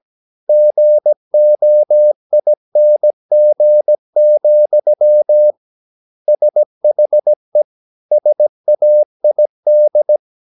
Morse code, heard as a keyed tone, about 17 wpm, 600 Hz. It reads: GOING, SHE SAID
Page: Romanen Never af Ken Follet på engelsk.